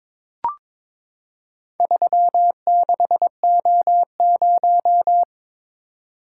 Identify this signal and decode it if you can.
Morse code, heard as a keyed tone, about 22 wpm, 700 Hz. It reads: 36O0
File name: practice / 22wpm / Four Letter-Number Combinations - 22wpm